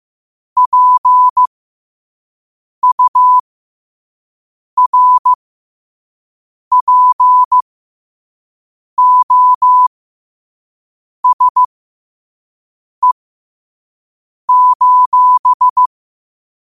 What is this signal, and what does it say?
Morse code, heard as a keyed tone, about 15 wpm, spaced as for 5 wpm, 1000 Hz. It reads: PURPOSE:
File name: qotdfiles / 5wpm